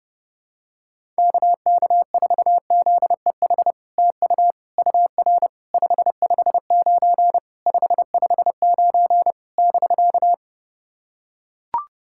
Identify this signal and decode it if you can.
Morse code, heard as a keyed tone, about 32 wpm, 700 Hz. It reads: KK4ZEH TU UR 559 559 <BK>
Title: SOTA QSO Practice 30wpm